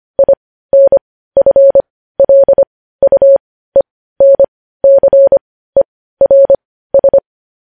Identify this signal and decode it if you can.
Morse code, heard as a keyed tone, about 25 wpm, spaced as for 15 wpm, 550 Hz. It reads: INFLUENCERS
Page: news headlines in morse code at 15 wpm